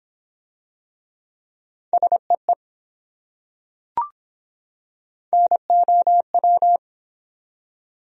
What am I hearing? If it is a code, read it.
Morse code, heard as a keyed tone, about 22 wpm, 700 Hz. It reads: SEE NOW